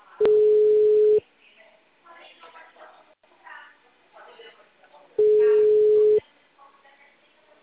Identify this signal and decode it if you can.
Ringback tone (Europe, 425 Hz), with speech.